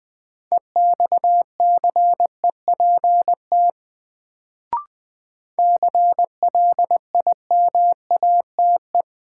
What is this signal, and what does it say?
Morse code, heard as a keyed tone, about 20 wpm, 700 Hz. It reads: EXCEPT CLIMATE
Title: Sentences from Top 1600 Words - New Word Drill - 20wpm